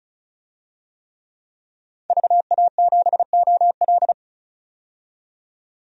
Morse code, heard as a keyed tone, about 36 wpm, 700 Hz. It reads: VA7OL
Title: DXCC-100 Call Signs - 5 Characters - 35wpm